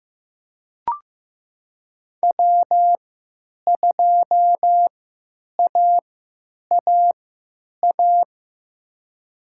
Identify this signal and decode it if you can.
Morse code, heard as a keyed tone, about 15 wpm, spaced as for 9 wpm, 700 Hz. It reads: W2AAA
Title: USA Call Signs - 5 Characters - 3x Character Spacing - 15wpm